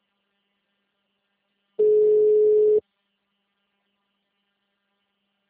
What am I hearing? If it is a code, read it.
Ringback tone (Europe, 425 Hz).
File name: out- -2007- - - .17777